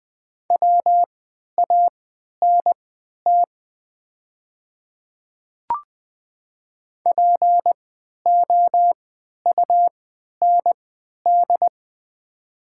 Morse code, heard as a keyed tone, about 20 wpm, spaced as for 11 wpm, 700 Hz. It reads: WANT POUND
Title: 3x Character Spacing - Top 500 Words - 20wpm